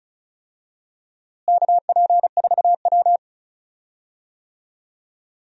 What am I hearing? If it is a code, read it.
Morse code, heard as a keyed tone, about 34 wpm, 700 Hz. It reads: KP4W
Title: DXCC-100 Call Signs - 4 Characters - Repeated 3x - 35wpm